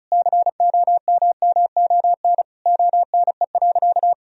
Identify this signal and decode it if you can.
Morse code, heard as a keyed tone, about 34 wpm, 700 Hz. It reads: COMMON ONE.
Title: Sentences from Top 1000 Words - Review Part 4 - 35wpm